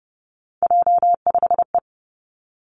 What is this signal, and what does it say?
Morse code, heard as a keyed tone, about 30 wpm, 700 Hz. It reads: J5E
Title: Three Challenging Letter-Number Combinations 30wpm